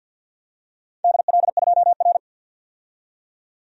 Morse code, heard as a keyed tone, about 53 wpm, 700 Hz. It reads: DL2R
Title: DXCC-100 Call Signs - 4 Characters - 50wpm